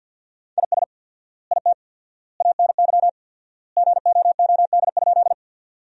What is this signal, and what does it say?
Morse code, heard as a keyed tone, about 50 wpm, 700 Hz. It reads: IS IT ANY GOOD?
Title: 4x Word-Spacing - Sentences from Top 200 Words - 50wpm